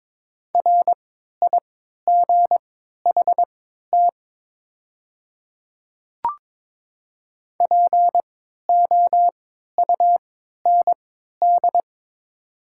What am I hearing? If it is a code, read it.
Morse code, heard as a keyed tone, about 22 wpm, spaced as for 13 wpm, 700 Hz. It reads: RIGHT POUND
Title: 3x Character Spacing - Top 500 Words - 22wpm